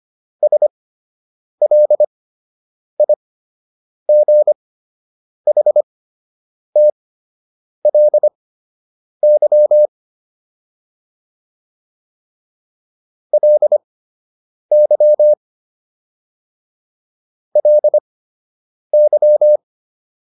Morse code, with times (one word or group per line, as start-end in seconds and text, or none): 0.4-9.9 SLIGHTLY
13.3-15.3 LY
17.6-19.6 LY